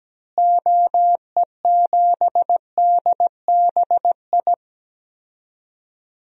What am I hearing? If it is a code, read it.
Morse code, heard as a keyed tone, about 17 wpm, 700 Hz. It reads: OE7DBI
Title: DXCC-100 Call Signs - 6 Characters - Repeated 3x - 17wpm